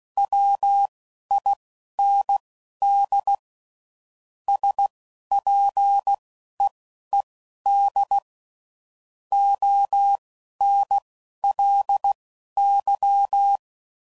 Morse code, heard as a keyed tone, about 16 wpm, spaced as for 11 wpm, 800 Hz. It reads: WIND SPEED ONLY
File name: CW Code Groups @10WPM / Weather Sets / Weather 5 1